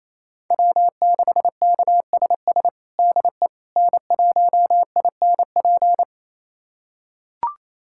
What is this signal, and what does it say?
Morse code, heard as a keyed tone, about 27 wpm, 690 Hz. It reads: W6KSS DE N1INP
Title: USA Callsign DE Callsign - 5 Characters - 28wpm